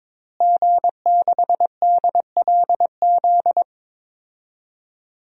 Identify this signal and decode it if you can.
Morse code, heard as a keyed tone, about 22 wpm, 700 Hz. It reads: G6DLZ
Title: DXCC-100 Call Signs - 5 Characters - Repeated 3x - 22wpm